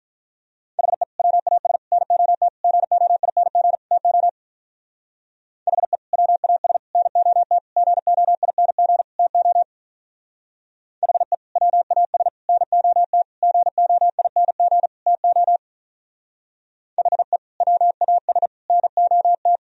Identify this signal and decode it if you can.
Morse code, heard as a keyed tone, about 44 wpm, 700 Hz. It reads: HE WAS NOT GOING TO HE WAS NOT GOING TO HE WAS NOT GOING TO HE WAS NOT
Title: Speed-Racing - Sets of 5 Words 35wpm